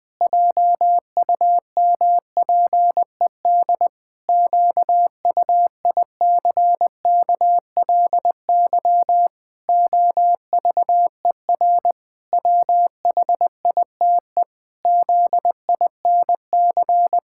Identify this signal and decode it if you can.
Morse code, heard as a keyed tone, about 20 wpm, 700 Hz. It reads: JUMPED QUICKLY OVER WHITE ZINC
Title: Pangram Challenge v2 - 20wpm